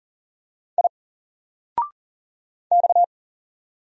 Morse code, heard as a keyed tone, about 40 wpm, 700 Hz. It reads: I X
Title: Single Character (Letter:Number:Punctuation:Prosign) 40wpm